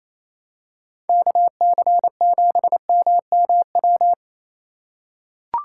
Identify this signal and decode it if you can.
Morse code, heard as a keyed tone, about 28 wpm, 700 Hz. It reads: KC7MMW
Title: USA Call Signs - 6 Characters - 2x Character Spacing - 28wpm